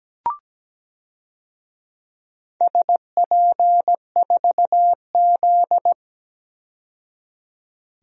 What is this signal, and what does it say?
Morse code, heard as a keyed tone, about 17 wpm, 700 Hz. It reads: SP4Z
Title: DXCC-100 Call Signs - 4 Characters - 17wpm